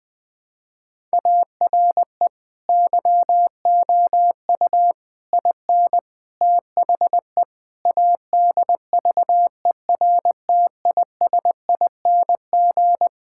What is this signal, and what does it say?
Morse code, heard as a keyed tone, about 20 wpm, 700 Hz. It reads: ARE YOU IN THE ADVERTISING
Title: Sentences from Top 1000 Words - Review Part 6 - 20wpm